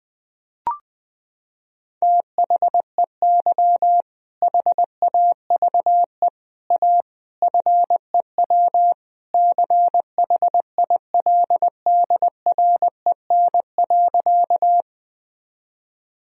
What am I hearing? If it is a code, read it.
Morse code, heard as a keyed tone, about 19 wpm, 700 Hz. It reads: THEY HAVE A FEW CHILDREN.
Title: Sentences from Top 1000 Words - Review Part 1 - 20wpm